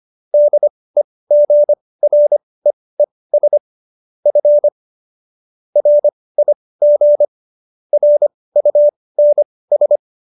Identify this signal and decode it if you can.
Morse code, heard as a keyed tone, about 25 wpm, spaced as for 18 wpm, 590 Hz. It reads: DEGREES F RIG RUNS